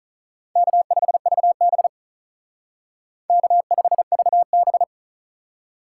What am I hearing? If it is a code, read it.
Morse code, heard as a keyed tone, about 43 wpm, 700 Hz. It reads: K5VB K5VB
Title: Speed Racing - USA 1x2 Call Signs 35wpm